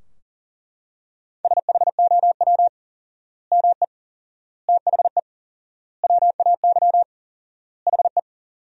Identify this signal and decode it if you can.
Morse code, heard as a keyed tone, about 40 wpm, 700 Hz. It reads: SHOW ME THE WAY HE
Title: 4x Word-Spacing - Sentences from Top 200 Words - 40wpm